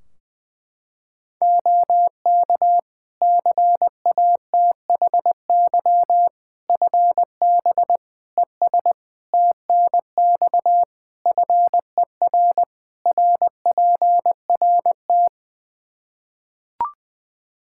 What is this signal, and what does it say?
Morse code, heard as a keyed tone, about 20 wpm, 700 Hz. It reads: OK CATHY FB ES TNX FER RPRT
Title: Practice QSO - Exchange 2 - Info - 20wpm